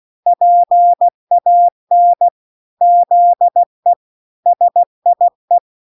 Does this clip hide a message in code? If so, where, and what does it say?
Yes: Morse code from 0.3 to 5.6 s, heard as a keyed tone, about 16 wpm, 700 Hz. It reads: PAN ZE SIE